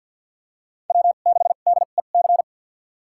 Morse code, heard as a keyed tone, about 47 wpm, spaced as for 35 wpm, 700 Hz. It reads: W6DEC
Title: USA Call Signs - 5 Characters - 2x Character Spacing - 50wpm